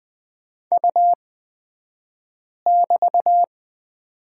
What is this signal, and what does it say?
Morse code, heard as a keyed tone, about 20 wpm, spaced as for 5 wpm, 700 Hz. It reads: U<BT>
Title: 223 Lesson 36d - BT - Two Characters 20wpm